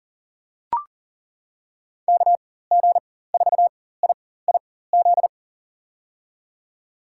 Morse code, heard as a keyed tone, about 39 wpm, spaced as for 18 wpm, 700 Hz. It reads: KG4IIZ
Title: USA Call Signs - 6 Characters - 4x Character Spacing - 40wpm